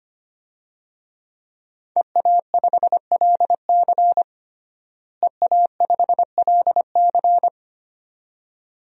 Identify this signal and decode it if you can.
Morse code, heard as a keyed tone, about 25 wpm, 700 Hz. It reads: EA5LC EA5LC
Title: DXCC-100 Call Signs - 5 Characters - Repeated 2x - 25wpm